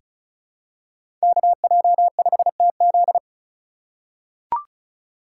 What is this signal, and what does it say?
Morse code, heard as a keyed tone, about 34 wpm, 700 Hz. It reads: KJ5TZ
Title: USA Call Signs - 5 Characters - 2x Character Spacing - 35wpm